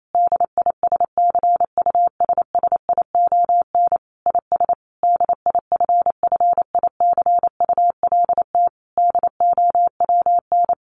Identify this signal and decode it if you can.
Morse code, heard as a keyed tone, about 28 wpm, 700 Hz. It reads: DISCUSSION IS DIFFICULT DOWN
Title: Sentences from Top 1100 Words 28wpm